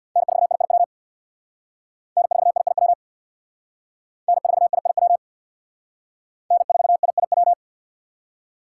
Morse code, heard as a keyed tone, about 60 wpm, 700 Hz. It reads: N4IIW N4IIW N4IIW N4IIW
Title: Speed Racing - USA 1x3 Call Signs 50wpm